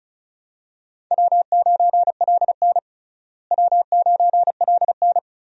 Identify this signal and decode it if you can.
Morse code, heard as a keyed tone, about 36 wpm, 700 Hz. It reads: W9LN W9LN
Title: USA Call Signs - 4 Characters - Repeated 3x - 35wpm